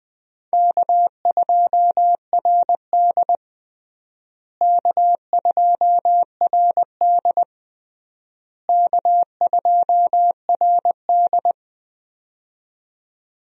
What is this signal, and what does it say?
Morse code, heard as a keyed tone, about 20 wpm, 700 Hz. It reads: K2RD K2RD K2RD
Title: USA Call Signs - 4 Characters - Repeated 3x - 20wpm